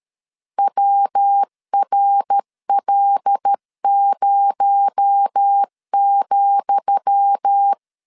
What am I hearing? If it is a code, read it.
Morse code, heard as a keyed tone, about 13 wpm, 800 Hz. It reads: WRL0,
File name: LEZIONI / 10 SIGNS / 10D COMA 12wpm 2544